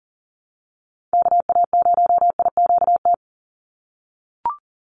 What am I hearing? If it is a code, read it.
Morse code, heard as a keyed tone, about 40 wpm, 700 Hz. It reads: KA0IQT
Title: Call Signs - 40wpm